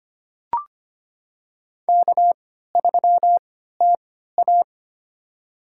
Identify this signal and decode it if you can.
Morse code, heard as a keyed tone, about 25 wpm, spaced as for 14 wpm, 700 Hz. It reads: K3TA
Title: USA Call Signs - 4 Characters - 3x Character Spacing - 25wpm